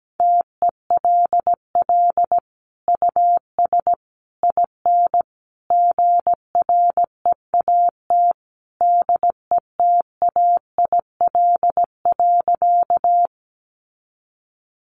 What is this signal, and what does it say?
Morse code, heard as a keyed tone, about 17 wpm, 700 Hz. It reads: TELL US IN GREAT DETAIL.
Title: Sentences from Top 1500 Words 17wpm